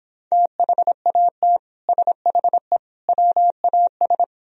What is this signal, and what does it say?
Morse code, heard as a keyed tone, about 26 wpm, 700 Hz. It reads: THAT SHE WAS